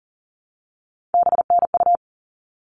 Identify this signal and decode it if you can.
Morse code, heard as a keyed tone, about 40 wpm, 700 Hz. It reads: BNU
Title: All Cut Numbers - 3-Digit 40wpm